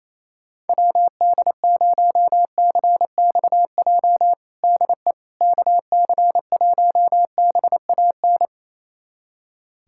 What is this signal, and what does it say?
Morse code, heard as a keyed tone, about 28 wpm, 700 Hz. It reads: WD0CXJ DE KC1BAN
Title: USA Callsign DE Callsign - 6 Characters - 28wpm